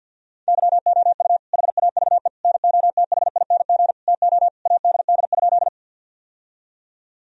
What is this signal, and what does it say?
Morse code, heard as a keyed tone, about 50 wpm, 700 Hz. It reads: YOU HAVE NOTHING TO ADD?
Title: Sentences from Top 1000 Words - Review Part 1 - 50wpm